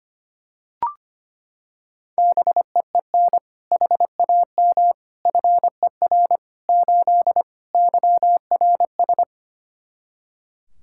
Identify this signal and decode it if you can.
Morse code, heard as a keyed tone, about 25 wpm, 700 Hz. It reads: BEEN HAM FER 8 YRS
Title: Practice QSO - Exchange 3 - More Info - 25wpm